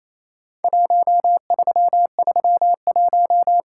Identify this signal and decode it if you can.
Morse code, heard as a keyed tone, about 28 wpm, 700 Hz. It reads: 1331
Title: Worked All Europe DX Contest - QTC Exchange - 28wpm